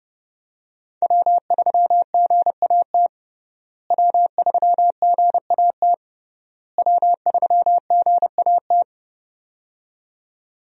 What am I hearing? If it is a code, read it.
Morse code, heard as a keyed tone, about 30 wpm, 700 Hz. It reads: W3GAT W3GAT W3GAT
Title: USA Call Signs - 5 Characters - Repeated 3x - 30wpm